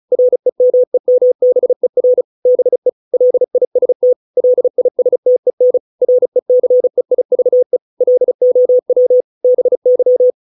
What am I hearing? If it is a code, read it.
Morse code, heard as a keyed tone, about 34 wpm, 500 Hz. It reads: REMEMBER BE LIST LISTEN RECEIVE LOW BY